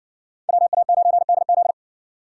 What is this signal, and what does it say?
Morse code, heard as a keyed tone, about 60 wpm, 700 Hz. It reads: WA9DZ